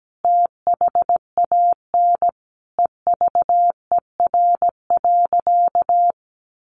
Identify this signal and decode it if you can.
Morse code, heard as a keyed tone, about 17 wpm, 700 Hz. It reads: THAN EVER.
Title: Sentences from Top 1400 Words 17wpm